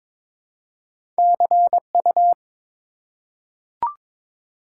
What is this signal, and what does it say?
Morse code, heard as a keyed tone, about 21 wpm, 700 Hz. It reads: CU
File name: Top 90 QSO Elements - 3x Character Spacing - 22wpm